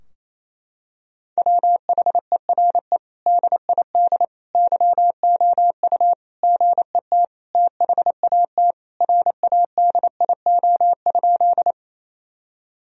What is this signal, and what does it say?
Morse code, heard as a keyed tone, about 28 wpm, 700 Hz. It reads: WHERE DID YOU GET THAT RADIO?
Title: Sentences from Top 1000 Words - Review Part 3 - 28wpm